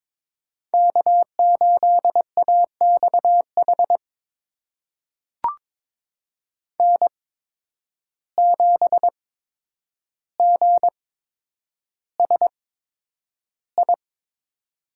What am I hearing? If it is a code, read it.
Morse code, heard as a keyed tone, about 22 wpm, spaced as for 9 wpm, 700 Hz. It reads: K8AXH N7GSI
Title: USA Call Signs - 5 Characters - 8x Character Spacing - 22wpm